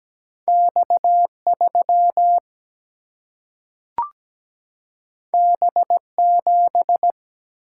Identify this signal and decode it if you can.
Morse code, heard as a keyed tone, about 17 wpm, 700 Hz. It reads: X3 B7
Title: Single-Letter Single-Number 17wpm